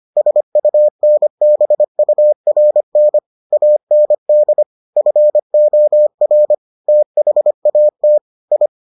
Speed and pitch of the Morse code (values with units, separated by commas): 25 wpm, 600 Hz